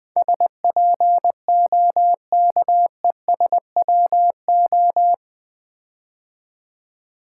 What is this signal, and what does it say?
Morse code, heard as a keyed tone, about 20 wpm, 700 Hz. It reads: SPOKESWO
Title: Build A Word - 4 Letters At A Time - 11-Letter Words - 20wpm